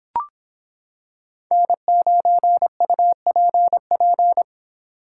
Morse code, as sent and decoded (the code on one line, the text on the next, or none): -. ----. ..- .--. .--.
N9UPP